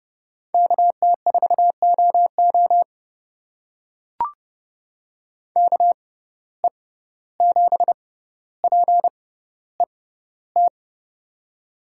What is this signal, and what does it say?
Morse code, heard as a keyed tone, about 31 wpm, spaced as for 15 wpm, 700 Hz. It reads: KT4OO KE7PET
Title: USA Call Signs - 6x Character Spacing - 30wpm